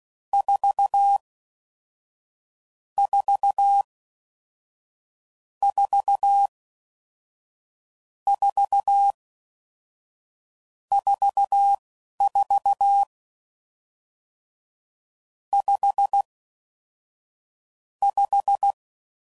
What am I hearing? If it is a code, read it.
Morse code, heard as a keyed tone, about 16 wpm, spaced as for 11 wpm, 800 Hz. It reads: 4 4 4 4 44 5 5